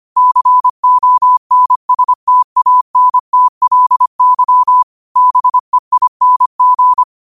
Morse code, heard as a keyed tone, about 25 wpm, 1000 Hz. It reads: CONSTANTLY BEING
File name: qotdfiles / 25wpm